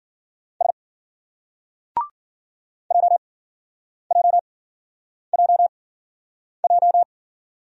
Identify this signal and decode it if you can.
Morse code, heard as a keyed tone, about 50 wpm, spaced as for 9 wpm, 700 Hz. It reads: S JJJJ
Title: Speed-Racing Single Letters 40wpm